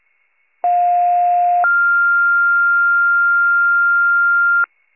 Two-tone page, heard as a one-second tone, then a three-second tone, about 710 then 1400 Hz. It is heard from 0.6 to 4.6 s.